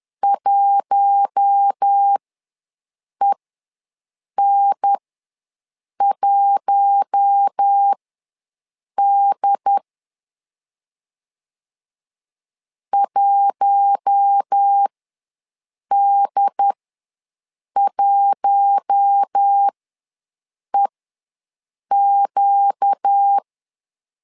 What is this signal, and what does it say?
Morse code, heard as a keyed tone, about 10 wpm, spaced as for 6 wpm, 800 Hz. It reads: 1EN1D 1D1EQ
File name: LEZIONI / 4 Q1 ws 10wpm 2538